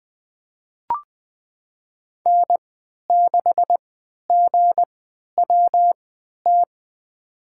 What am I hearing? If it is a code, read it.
Morse code, heard as a keyed tone, about 21 wpm, spaced as for 12 wpm, 700 Hz. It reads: N6GWT
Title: USA Call Signs - 5 Characters - 3x Character Spacing - 20wpm